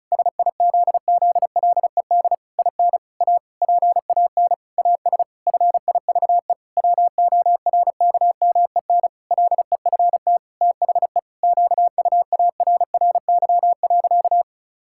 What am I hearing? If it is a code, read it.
Morse code, heard as a keyed tone, about 36 wpm, 700 Hz. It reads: SIZZLED IN A PAN AS FIVE WORKMEN LEFT THE QUARRY.
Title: Pangram Challenge v2 - 35wpm